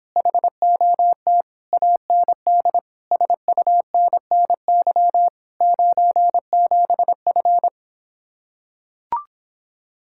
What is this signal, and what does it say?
Morse code, heard as a keyed tone, about 26 wpm, 700 Hz. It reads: HOT AND SUNNY 97F